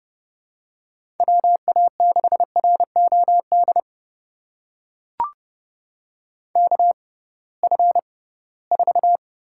Morse code, heard as a keyed tone, about 30 wpm, spaced as for 19 wpm, 700 Hz. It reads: WA6ROD KF4
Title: USA Call Signs - 6 Characters - 6x Character Spacing - 30wpm